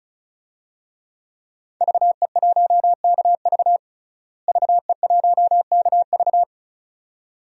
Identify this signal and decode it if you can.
Morse code, heard as a keyed tone, about 36 wpm, 700 Hz. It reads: VE1KV VE1KV
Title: DXCC-100 Call Signs - 5 Characters - Repeated 2x - 35wpm